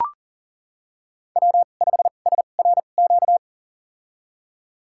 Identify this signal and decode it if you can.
Morse code, heard as a keyed tone, about 39 wpm, spaced as for 29 wpm, 700 Hz. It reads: W5SRQ
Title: USA Call Signs - 5 Characters - 2x Character Spacing - 40wpm